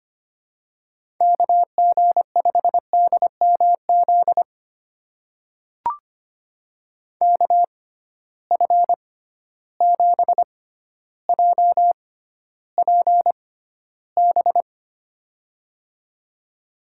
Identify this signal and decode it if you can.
Morse code, heard as a keyed tone, about 25 wpm, spaced as for 13 wpm, 700 Hz. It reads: KG5DMZ KF7JPB
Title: USA Call Signs - 6 Characters - 6x Character Spacing - 25wpm